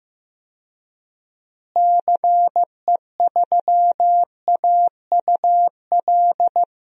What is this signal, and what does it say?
Morse code, heard as a keyed tone, about 15 wpm, 700 Hz. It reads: CE3AUL
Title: DXCC-100 Call Signs - 6 Characters - Repeated 2x - 15wpm